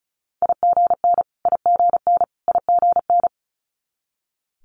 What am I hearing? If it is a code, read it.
Morse code, heard as a keyed tone, about 36 wpm, 700 Hz. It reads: IGN IGN IGN
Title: Multi-letter Phonemes Repeated 3x 35wpm